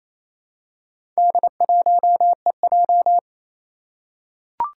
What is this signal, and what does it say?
Morse code, heard as a keyed tone, about 29 wpm, 700 Hz. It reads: D1EJ
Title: Four Challenging Letter-Number Combinations 28wpm